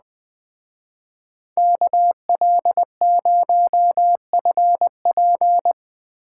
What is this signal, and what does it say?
Morse code, heard as a keyed tone, about 20 wpm, 700 Hz. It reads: KL0FP